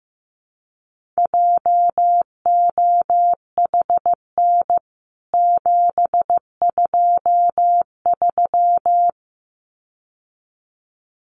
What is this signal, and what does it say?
Morse code, heard as a keyed tone, about 15 wpm, 700 Hz. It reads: JOHN 723
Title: CWT Exchange 15wpm